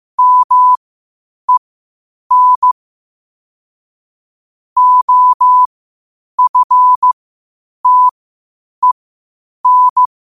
Morse code, heard as a keyed tone, about 15 wpm, spaced as for 8 wpm, 1000 Hz. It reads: MEN OFTEN
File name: qotdfiles / 8wpm